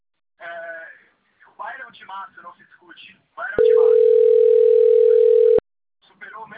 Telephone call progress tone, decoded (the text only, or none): ringback tone